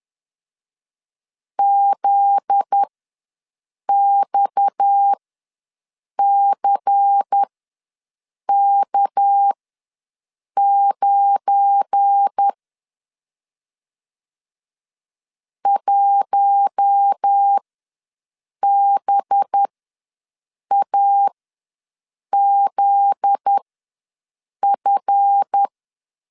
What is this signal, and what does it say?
Morse code, heard as a keyed tone, about 10 wpm, spaced as for 6 wpm, 800 Hz. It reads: ZXCK9 1BAZF
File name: LEZIONI / 7 GF9Z / 7 GF9Z ws 10wpm 3919